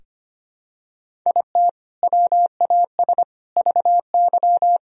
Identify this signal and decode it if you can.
Morse code, heard as a keyed tone, about 25 wpm, 700 Hz. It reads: IT WAS VY
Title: Sentences from Top 300 Words and Abbreviations 25wpm